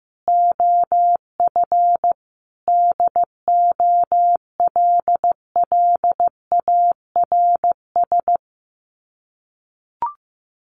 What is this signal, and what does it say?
Morse code, heard as a keyed tone, about 15 wpm, 700 Hz. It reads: OF DOLLARS
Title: Sets of 5 Words Encore2 15wpm